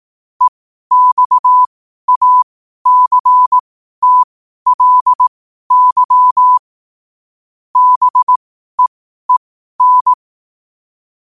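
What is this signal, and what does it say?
Morse code, heard as a keyed tone, about 18 wpm, spaced as for 12 wpm, 1000 Hz. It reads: EXACTLY BEEN
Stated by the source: 12 wpm Morse.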